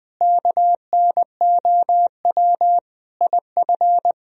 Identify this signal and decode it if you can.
Morse code, heard as a keyed tone, about 22 wpm, 700 Hz. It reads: KNOW IF
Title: Speed-Racing - Sets of 5 Words 20wpm